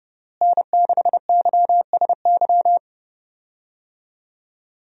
Morse code, heard as a keyed tone, about 30 wpm, 700 Hz. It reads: N6YSY
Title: USA Call Signs - 5 Characters - Repeated 2x - 30wpm